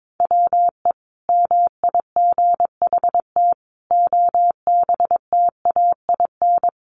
Morse code, heard as a keyed tone, about 22 wpm, 700 Hz. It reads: WE MIGHT OBTAIN
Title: Sentences from Top 1300 Words 22wpm